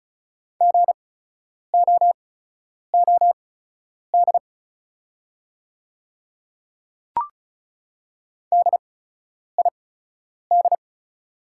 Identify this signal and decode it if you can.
Morse code, heard as a keyed tone, about 34 wpm, spaced as for 10 wpm, 690 Hz. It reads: GOOD DID
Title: 8x Character Spacing - Top 200 Words - 35wpm